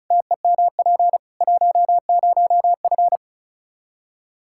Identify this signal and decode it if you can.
Morse code, heard as a keyed tone, about 36 wpm, 700 Hz. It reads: TEMP 10F
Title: Practice QSO - Exchange 2 - Info - 35wpm